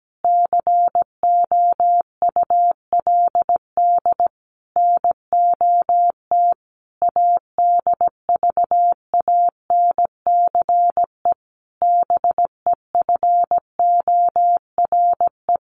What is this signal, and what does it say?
Morse code, heard as a keyed tone, about 17 wpm, 700 Hz. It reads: COULD NOT ADVANCE BEFORE
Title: Sentences from Top 1400 Words 17wpm